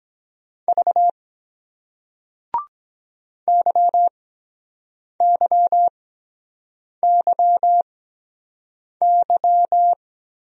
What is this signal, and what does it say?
Morse code, heard as a keyed tone, about 23 wpm, spaced as for 7 wpm, 700 Hz. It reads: V YYYY